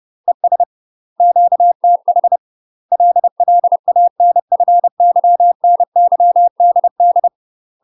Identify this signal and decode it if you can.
Morse code, heard as a keyed tone, about 30 wpm, 700 Hz. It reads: ES QTH LLANFYNYDD
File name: UK MASTER QSOs TRAINING / 30WPM / UK MASTER QSOs 0026